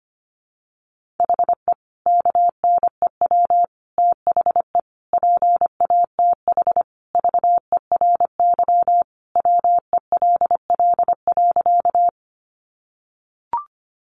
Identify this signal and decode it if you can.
Morse code, heard as a keyed tone, about 25 wpm, 700 Hz. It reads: HE KNEW THE PATH VERY WELL.
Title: Sentences from Top 1400 Words 25wpm